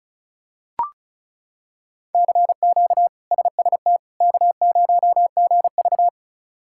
Morse code, heard as a keyed tone, about 34 wpm, 700 Hz. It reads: CQ SST K0GV
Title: K1USN SST QSO Practice - 35wpm